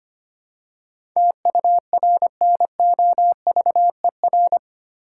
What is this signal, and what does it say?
Morse code, heard as a keyed tone, about 25 wpm, 700 Hz. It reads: TURNOVER
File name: Eight Letter Words - 25wpm